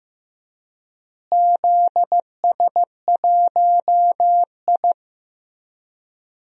Morse code, heard as a keyed tone, about 15 wpm, 700 Hz. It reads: ZS1I
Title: DXCC-100 Call Signs - 4 Characters - Repeated 3x - 15wpm